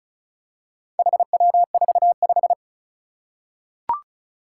Morse code, heard as a keyed tone, about 36 wpm, 700 Hz. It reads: HW45
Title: Four Challenging Letter-Number Combinations 35wpm